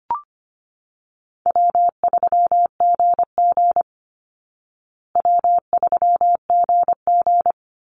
Morse code, heard as a keyed tone, about 25 wpm, 700 Hz. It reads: W3GG W3GG
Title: Easy Call Signs Repeated 2x - 25wpm